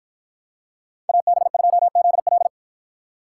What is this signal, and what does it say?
Morse code, heard as a keyed tone, about 53 wpm, 700 Hz. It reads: AB2ZL